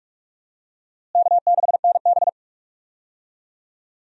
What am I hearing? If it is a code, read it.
Morse code, heard as a keyed tone, about 42 wpm, 700 Hz. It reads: K6NB
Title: USA Call Signs - 4 Characters - Repeated 2x - 45wpm